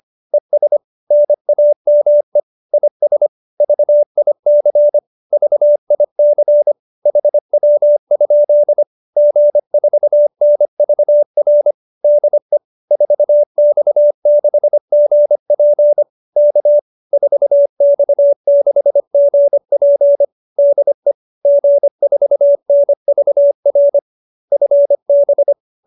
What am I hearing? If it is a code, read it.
Morse code, heard as a keyed tone, about 25 wpm, 590 Hz. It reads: ES NAME IS VIC VIC HW? G4NVR DE 4X6GP K 4X6GP DE G4NVR FB
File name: QSO315 25